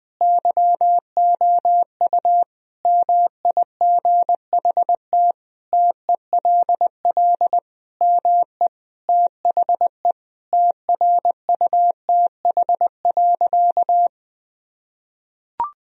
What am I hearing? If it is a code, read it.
Morse code, heard as a keyed tone, about 21 wpm, 700 Hz. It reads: YOU MIGHT TELL ME THE TRUTH.
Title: Sentences from Top 1000 Words - Review Part 1 - 20wpm